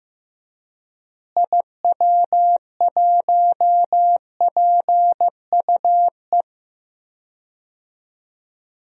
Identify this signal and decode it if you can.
Morse code, heard as a keyed tone, about 15 wpm, 700 Hz. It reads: IW1PUE
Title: DXCC-100 Call Signs - 6 Characters - Repeated 3x - 15wpm